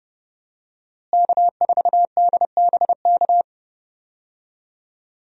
Morse code, heard as a keyed tone, about 31 wpm, 700 Hz. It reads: K4DBK
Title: USA Call Signs - 5 Characters - Repeated 2x - 30wpm